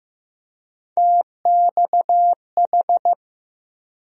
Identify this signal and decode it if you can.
Morse code, heard as a keyed tone, about 15 wpm, 700 Hz. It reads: TXH
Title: USA Call Sign Postfixes - 3 Letters 15wpm